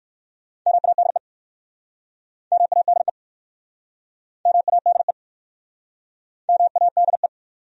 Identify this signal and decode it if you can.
Morse code, heard as a keyed tone, about 57 wpm, 700 Hz. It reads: MADE MADE MADE MADE